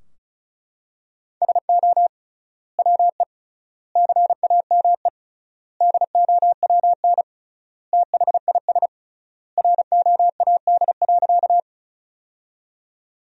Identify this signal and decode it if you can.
Morse code, heard as a keyed tone, about 34 wpm, 700 Hz. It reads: SO WE CAME DOWN THIS ROAD.
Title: 3x Word-Spacing - Sentences from Top 400 Words - 35wpm